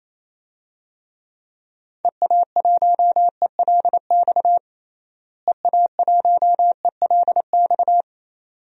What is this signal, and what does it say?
Morse code, heard as a keyed tone, about 28 wpm, 700 Hz. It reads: EA1ELX EA1ELX
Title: DXCC-100 Call Signs - 6 Characters - Repeated 2x - 28wpm